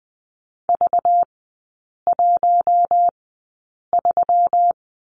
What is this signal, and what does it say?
Morse code, heard as a keyed tone, about 20 wpm, spaced as for 8 wpm, 700 Hz. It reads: V13
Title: 218 Lesson 35e - Number 0 - Three Characters 20wpm